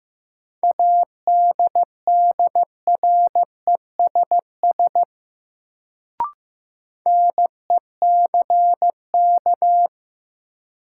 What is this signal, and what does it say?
Morse code, heard as a keyed tone, about 15 wpm, 700 Hz. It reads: ADDRESS NECK
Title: Sets of 5 Words Encore - New Word Drill - 15wpm